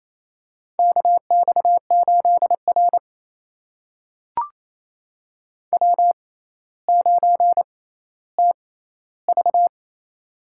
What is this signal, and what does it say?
Morse code, heard as a keyed tone, about 28 wpm, spaced as for 14 wpm, 700 Hz. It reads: KX8R W9TV
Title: USA Call Signs - 4 Characters - 6x Character Spacing - 28wpm